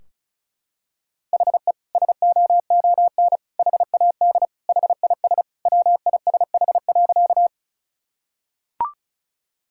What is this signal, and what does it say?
Morse code, heard as a keyed tone, about 36 wpm, 700 Hz. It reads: HE SOON HAD HIS WISH.
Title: Sentences from Top 1600 Words 35wpm